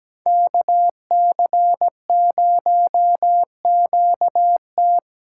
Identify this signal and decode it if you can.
Morse code, heard as a keyed tone, about 19 wpm, 700 Hz. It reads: KC0QT
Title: Speed Racing - USA Call Signs 17wpm